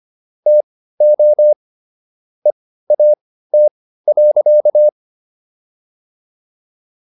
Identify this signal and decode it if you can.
Morse code, heard as a keyed tone, about 25 wpm, spaced as for 15 wpm, 600 Hz. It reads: TO EAT.